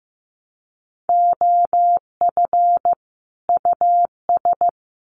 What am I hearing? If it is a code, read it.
Morse code, heard as a keyed tone, about 15 wpm, 700 Hz. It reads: OF US
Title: Sets of 2 Words 15wpm